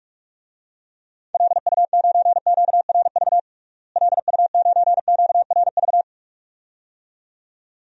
Morse code, heard as a keyed tone, about 41 wpm, 700 Hz. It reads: LU9QRV LU9QRV
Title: DXCC-100 Call Signs - 6 Characters - Repeated 2x - 45wpm